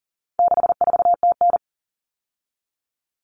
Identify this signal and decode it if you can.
Morse code, heard as a keyed tone, about 40 wpm, 700 Hz. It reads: 64TN
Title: All Cut Numbers - 4-Digit 40wpm